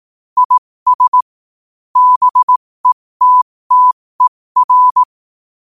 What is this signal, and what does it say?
Morse code, heard as a keyed tone, about 18 wpm, spaced as for 15 wpm, 1000 Hz. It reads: IS BETTER
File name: qotdfiles / 15wpm